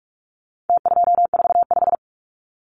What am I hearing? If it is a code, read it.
Morse code, heard as a keyed tone, about 46 wpm, 700 Hz. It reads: T245
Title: Numbers - Contest Style - 4-Digit 45wpm